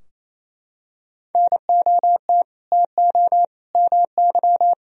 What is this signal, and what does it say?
Morse code, heard as a keyed tone, about 28 wpm, 700 Hz. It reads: NOT TO MY
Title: Sentences from Top 1000 Words - Review Part 3 - 28wpm